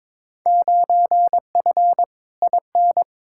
Morse code, heard as a keyed tone, about 22 wpm, 700 Hz. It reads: 9F IN
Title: Field Day Exchanges Encore - 22wpm